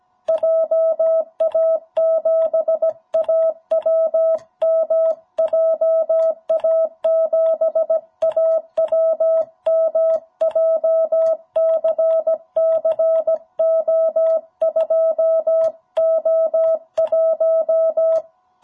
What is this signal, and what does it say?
Morse code, heard as a keyed tone, about 17 wpm, 650 Hz. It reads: JA7AWM JA7AWM JCC O2O1
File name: jh8cbh / 2021cw